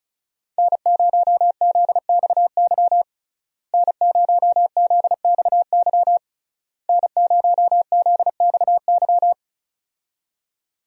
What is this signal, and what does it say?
Morse code, heard as a keyed tone, about 34 wpm, 700 Hz. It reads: N0ZXY N0ZXY N0ZXY
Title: USA Call Signs - 5 Characters - Repeated 3x - 35wpm